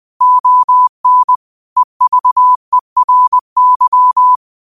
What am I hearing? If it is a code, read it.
Morse code, heard as a keyed tone, about 20 wpm, 1000 Hz. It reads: ON EVERY